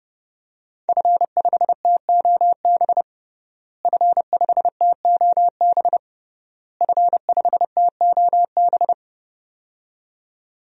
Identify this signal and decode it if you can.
Morse code, heard as a keyed tone, about 31 wpm, 700 Hz. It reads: F5TOB F5TOB F5TOB
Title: DXCC-100 Call Signs - 5 Characters - Repeated 3x - 30wpm